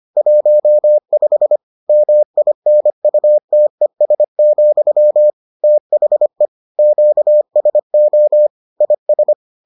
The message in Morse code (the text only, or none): 15 MINUTES, THE QSO IS